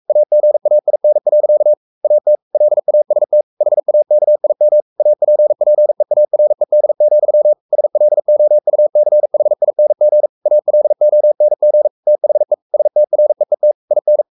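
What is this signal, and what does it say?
Morse code, heard as a keyed tone, about 44 wpm, 600 Hz. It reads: AGAIN. AT LAST HAKIM APPEARED, SLOUCHING ALONG THE STREET IN
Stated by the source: Morse code at 43 wpm sending Ken Follett's novel Never in English, from CW med Gnister.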